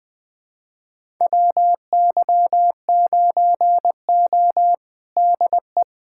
Morse code, heard as a keyed tone, about 20 wpm, 700 Hz. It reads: WY9O DE
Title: USA Callsign DE Callsign - 4 Characters - 20wpm